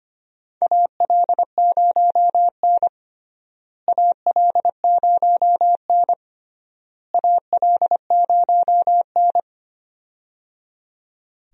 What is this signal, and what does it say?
Morse code, heard as a keyed tone, about 25 wpm, 700 Hz. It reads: AL0N AL0N AL0N
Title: USA Call Signs - 4 Characters - Repeated 3x - 25wpm